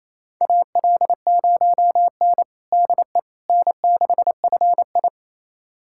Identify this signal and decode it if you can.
Morse code, heard as a keyed tone, about 28 wpm, 700 Hz. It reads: AL0N DE N6FI
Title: USA Callsign DE Callsign - 4 Characters - 28wpm